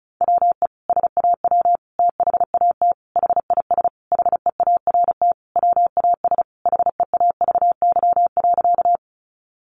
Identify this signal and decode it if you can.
Morse code, heard as a keyed tone, about 34 wpm, 700 Hz. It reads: WE SAW THAT HIS HEART WAS HEAVY.
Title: Sentences from Top 1200 Words 35wpm